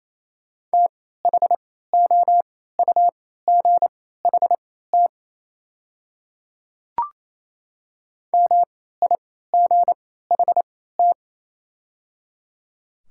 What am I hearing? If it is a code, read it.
Morse code, heard as a keyed tone, about 28 wpm, spaced as for 16 wpm, 700 Hz. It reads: THOUGHT MIGHT
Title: 3x Character Spacing - Top 300 Words - 28wpm